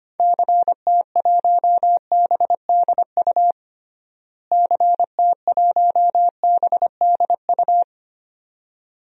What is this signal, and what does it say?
Morse code, heard as a keyed tone, about 25 wpm, 700 Hz. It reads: CT1BDU CT1BDU
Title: DXCC-100 Call Signs - 6 Characters - Repeated 2x - 25wpm